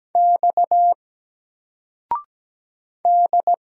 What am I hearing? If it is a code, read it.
Morse code, heard as a keyed tone, about 17 wpm, 700 Hz. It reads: X D
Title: Single Letter-Number 17wpm